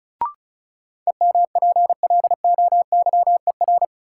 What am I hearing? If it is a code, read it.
Morse code, heard as a keyed tone, about 36 wpm, 700 Hz. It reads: EMPLOYER
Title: Sentences from Top 1600 Words - New Word Drill - 35wpm